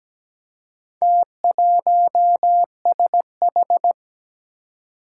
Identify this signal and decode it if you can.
Morse code, heard as a keyed tone, about 17 wpm, 700 Hz. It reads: T1SH
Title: Four Challenging Letter-Number Combinations 17wpm